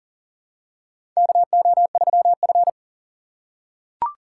Morse code, heard as a keyed tone, about 40 wpm, 700 Hz. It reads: KO3F
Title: USA Call Signs - 4 Characters - 2x Character Spacing - 40wpm